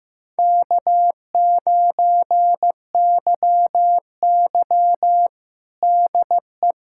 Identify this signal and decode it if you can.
Morse code, heard as a keyed tone, about 15 wpm, 700 Hz. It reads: K9YY DE
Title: USA Callsign DE Callsign - 4 Characters - 15wpm